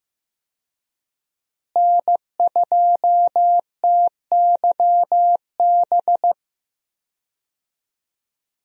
Morse code, heard as a keyed tone, about 16 wpm, 700 Hz. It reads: N2TYB
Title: USA Call Signs - 5 Characters - Repeated 3x - 15wpm